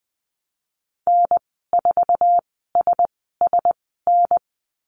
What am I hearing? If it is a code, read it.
Morse code, heard as a keyed tone, about 20 wpm, spaced as for 15 wpm, 700 Hz. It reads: N4SSN
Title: 055 Lesson 10g - Letter r - Call Signs 20wpm